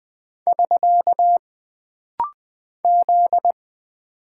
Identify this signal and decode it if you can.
Morse code, heard as a keyed tone, about 20 wpm, 700 Hz. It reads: <SK> Z
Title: Single Character (Letter:Number:Punctuation:Prosign) - Rapid-Fire - 20wpm